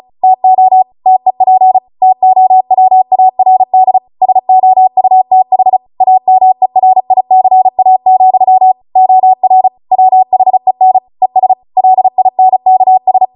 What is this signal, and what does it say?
Morse code, heard as a keyed tone, about 36 wpm, 750 Hz. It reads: TO TEP TOWARD SOUTH AMERICA, OR WHEN ES LINKS